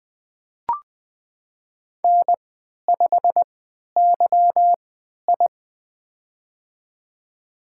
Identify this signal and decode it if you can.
Morse code, heard as a keyed tone, about 20 wpm, spaced as for 11 wpm, 700 Hz. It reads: N5YI
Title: USA Call Signs - 4 Characters - 3x Character Spacing - 20wpm